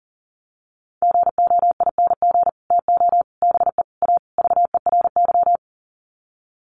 Morse code, heard as a keyed tone, about 37 wpm, 700 Hz. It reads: GOING TO BE A VERY
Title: Sets of 5 Words Encore 40wpm